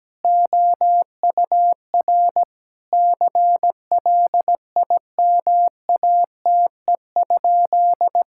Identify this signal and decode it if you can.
Morse code, heard as a keyed tone, about 17 wpm, 700 Hz. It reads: OUR CLIMATE?
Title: Sentences from Top 1600 Words 17wpm